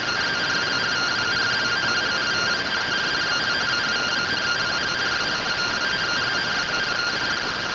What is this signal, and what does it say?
Two-tone teleprinter signal, about 170 Hz shift.